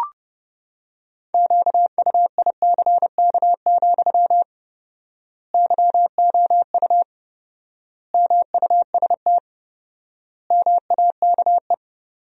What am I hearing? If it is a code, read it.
Morse code, heard as a keyed tone, about 30 wpm, 700 Hz. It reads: QUICK, YOU MUST MAKE
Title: 4x Word-Spacing - Sentences from Top 500 Words - 30wpm